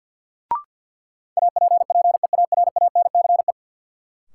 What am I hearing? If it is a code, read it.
Morse code, heard as a keyed tone, about 52 wpm, 700 Hz. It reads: APPEARANCE
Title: Sentences from Top 1100 Words - New Word Drill - 50wpm